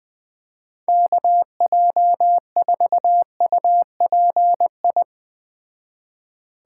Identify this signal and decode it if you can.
Morse code, heard as a keyed tone, about 20 wpm, 700 Hz. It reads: KJ4UPI